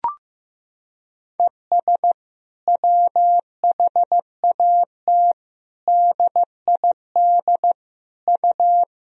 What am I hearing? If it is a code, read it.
Morse code, heard as a keyed tone, about 15 wpm, 700 Hz. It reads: ES WHAT DID U
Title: Sentences from Top 100 Words and Abbreviations 15wpm